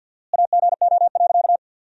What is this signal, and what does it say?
Morse code, heard as a keyed tone, about 53 wpm, 700 Hz. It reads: AGO.
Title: 4x Word-Spacing - Sentences from Top 500 Words - 50wpm